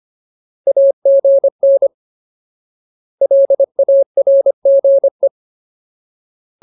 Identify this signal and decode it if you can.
Morse code, heard as a keyed tone, about 25 wpm, 550 Hz. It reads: AGN LARGE
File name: Ultimate Word List / 25WPM Ultimate